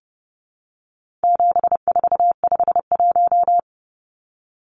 Morse code, heard as a keyed tone, about 30 wpm, 700 Hz. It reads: 7451
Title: Numbers - 4-Digit 30wpm